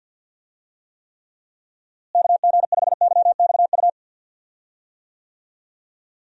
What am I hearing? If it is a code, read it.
Morse code, heard as a keyed tone, about 46 wpm, 700 Hz. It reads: KG5YXU
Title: USA Call Signs - 6 Characters - Repeated 3x - 50wpm